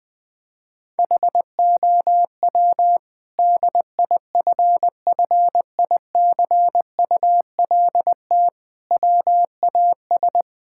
Morse code, heard as a keyed tone, about 20 wpm, 700 Hz. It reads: HOW DIFFICULT WAS
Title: Sentences from Top 1000 Words - Review Part 6 - 20wpm